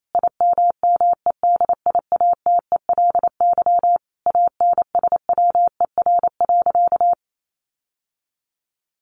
Morse code, heard as a keyed tone, about 28 wpm, 700 Hz. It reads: IMMEDIATELY ANSWER.
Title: Sentences from Top 1100 Words 28wpm